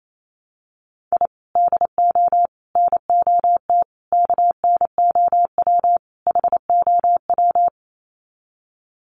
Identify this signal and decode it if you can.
Morse code, heard as a keyed tone, about 28 wpm, 700 Hz. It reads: I DO NOT KNOW HOW
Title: Sets of 5 Words 28wpm